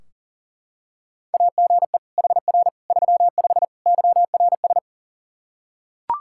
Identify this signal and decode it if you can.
Morse code, heard as a keyed tone, about 37 wpm, 700 Hz. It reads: AGE HR 35 YRS
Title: K1USN SST QSO Practice 40wpm